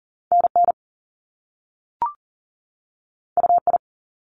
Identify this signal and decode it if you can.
Morse code, heard as a keyed tone, about 39 wpm, 700 Hz. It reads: NN UI
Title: Top Two Letter Combinations in Words 40wpm